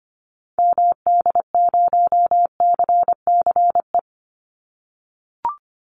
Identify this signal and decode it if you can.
Morse code, heard as a keyed tone, about 25 wpm, 700 Hz. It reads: MD0CCE
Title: Call Signs - 25wpm